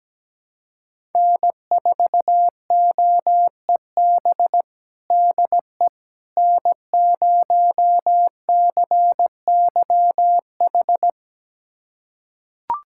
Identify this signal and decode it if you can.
Morse code, heard as a keyed tone, about 17 wpm, 700 Hz. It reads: N4OEB DE N0CYH
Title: USA Callsign DE Callsign 17wpm